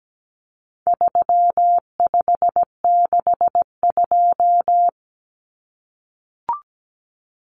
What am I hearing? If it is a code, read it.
Morse code, heard as a keyed tone, about 17 wpm, 700 Hz. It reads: 3562
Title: Numbers - 4-Digit 17wpm